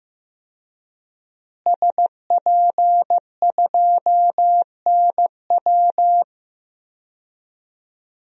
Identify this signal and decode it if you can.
Morse code, heard as a keyed tone, about 15 wpm, 700 Hz. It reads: SP2NW
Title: DXCC-100 Call Signs - 5 Characters - Repeated 2x - 15wpm